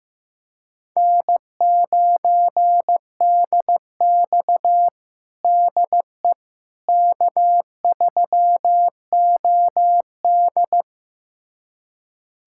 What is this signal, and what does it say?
Morse code, heard as a keyed tone, about 15 wpm, 700 Hz. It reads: N9DX DE K3OD
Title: USA Callsign DE Callsign - 4 Characters - 15wpm